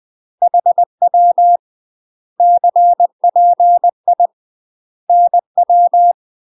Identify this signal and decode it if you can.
Morse code, heard as a keyed tone, about 20 wpm, 700 Hz. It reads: HW CPI NW
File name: UK MASTER QSOs TRAINING / 20WPM / UK MASTER QSOs 0047